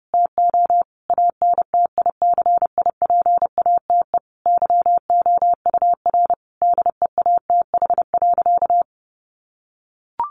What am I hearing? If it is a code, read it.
Morse code, heard as a keyed tone, about 30 wpm, 700 Hz. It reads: TO ANTICIPATE YOUR DEATH.
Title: Sentences from Top 1500 Words 30wpm